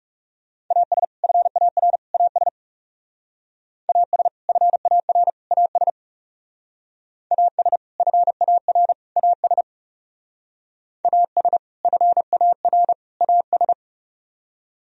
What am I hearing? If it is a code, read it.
Morse code, heard as a keyed tone, about 40 wpm, 700 Hz. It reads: AS FAR AS AS FAR AS AS FAR AS AS FAR AS